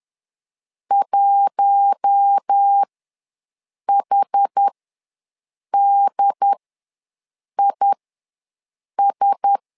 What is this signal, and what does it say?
Morse code, heard as a keyed tone, about 10 wpm, spaced as for 6 wpm, 800 Hz. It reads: 1HDIS
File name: LEZIONI / 4 Q1S ws 10wpm 2208